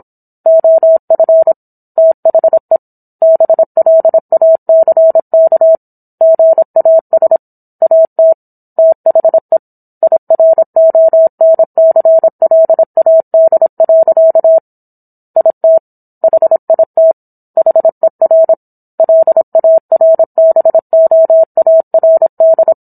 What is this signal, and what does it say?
Morse code, heard as a keyed tone, about 27 wpm, 650 Hz. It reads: OF THE BLACK GAS AT THE IRONCLAD. IT HIT HER LARBOARD